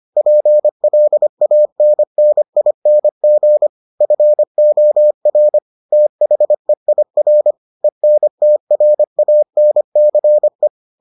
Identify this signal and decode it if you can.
Morse code, heard as a keyed tone, about 25 wpm, 600 Hz. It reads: PLANNING FOR THEIR ENTRANCE